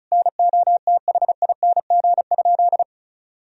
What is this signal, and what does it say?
Morse code, heard as a keyed tone, about 34 wpm, 700 Hz. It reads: NOTHING?
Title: Sentences from Top 500 Words and Abbreviations 35wpm